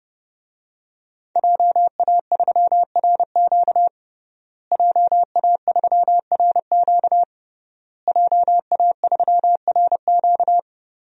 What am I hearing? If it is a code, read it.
Morse code, heard as a keyed tone, about 32 wpm, 700 Hz. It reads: JA3RQ JA3RQ JA3RQ
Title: DXCC-100 Call Signs - 5 Characters - Repeated 3x - 30wpm